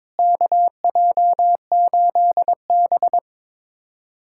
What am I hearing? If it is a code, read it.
Morse code, heard as a keyed tone, about 22 wpm, 700 Hz. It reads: KJ8B
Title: USA Callsign DE Callsign - 4 Characters - 22wpm